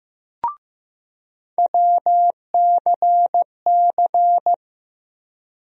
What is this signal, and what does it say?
Morse code, heard as a keyed tone, about 15 wpm, 700 Hz. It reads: WCC
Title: USA Call Sign Postfixes - 3 Letters 15wpm